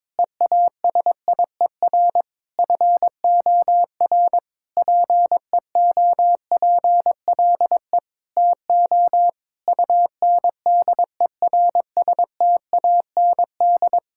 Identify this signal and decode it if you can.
Morse code, heard as a keyed tone, about 21 wpm, 700 Hz. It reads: EASIER FOR PEOPLE TO UNDERSTAND
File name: Sentences with 7-Word Phrases - 22wpm